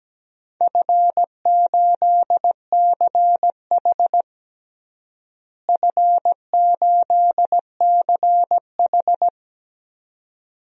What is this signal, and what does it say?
Morse code, heard as a keyed tone, about 17 wpm, 700 Hz. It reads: F8CH F8CH
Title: DXCC-100 Call Signs - 4 Characters - Repeated 2x - 17wpm